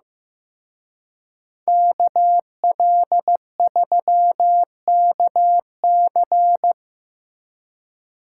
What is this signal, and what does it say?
Morse code, heard as a keyed tone, about 17 wpm, 700 Hz. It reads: KL3KC
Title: Speed Racing - USA 2x2 Call Signs 15wpm